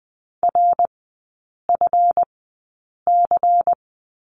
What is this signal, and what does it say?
Morse code, heard as a keyed tone, about 20 wpm, spaced as for 8 wpm, 690 Hz. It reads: RFC
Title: 145 Lesson 24e - Letter y - Three Characters 20wpm